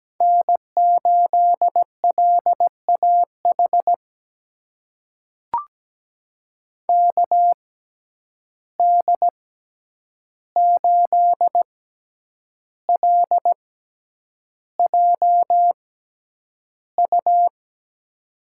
Morse code, heard as a keyed tone, about 17 wpm, spaced as for 8 wpm, 700 Hz. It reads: N8LAH KD8LJU
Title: USA Call Signs - 6x Character Spacing - 17wpm